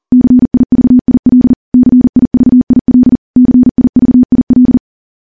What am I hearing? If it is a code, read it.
Morse code, heard as a keyed tone, about 40 wpm, 250 Hz. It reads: CIVIL CIVIL CIVIL
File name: morse / morse-training-40WPM